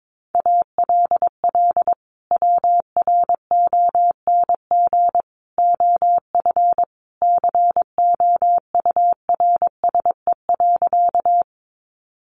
Morse code, heard as a keyed tone, about 22 wpm, 700 Hz. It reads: ALL WRONG OF COURSE.
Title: Sentences from Top 900 Words 22wpm